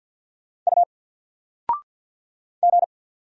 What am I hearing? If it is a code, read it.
Morse code, heard as a keyed tone, about 48 wpm, 700 Hz. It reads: U G
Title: Single Letter-Number Alt 50wpm